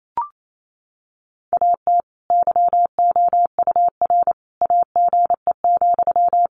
Morse code, heard as a keyed tone, about 28 wpm, 700 Hz. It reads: AT YOUR AGE,
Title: Sentences from Top 1500 Words 28wpm